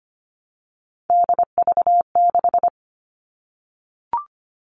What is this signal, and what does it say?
Morse code, heard as a keyed tone, about 25 wpm, 700 Hz. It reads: DV6
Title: All Cut Numbers - 3-Digit 25wpm